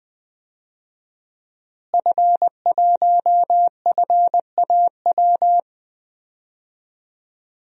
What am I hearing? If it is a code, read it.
Morse code, heard as a keyed tone, about 20 wpm, 700 Hz. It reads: F1FAW
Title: DXCC-100 Call Signs - 5 Characters - Repeated 3x - 20wpm